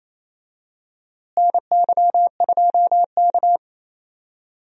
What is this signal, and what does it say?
Morse code, heard as a keyed tone, about 28 wpm, 700 Hz. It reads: NY2K
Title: USA Call Signs - 4 Characters - Repeated 3x - 28wpm